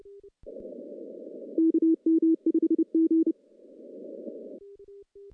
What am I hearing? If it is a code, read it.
Morse code, heard as a keyed tone, about 29 wpm, 320 Hz. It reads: KM5G
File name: contests / CW-SS-k5tr- -1400